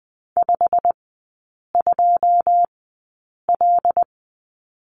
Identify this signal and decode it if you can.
Morse code, heard as a keyed tone, about 20 wpm, spaced as for 8 wpm, 700 Hz. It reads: 52L
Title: 127 Lesson 21e - Number 6 - Three Characters 20wpm